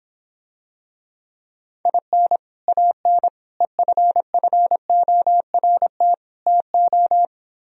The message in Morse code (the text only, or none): IN AN EFFORT TO